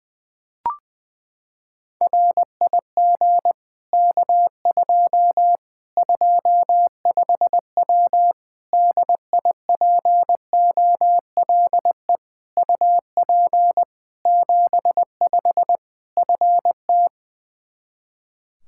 Morse code, heard as a keyed tone, about 20 wpm, 700 Hz. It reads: RIG K2 25W DIPOLE UP 75 FT
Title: Practice QSO - Exchange 2 - Info - 20wpm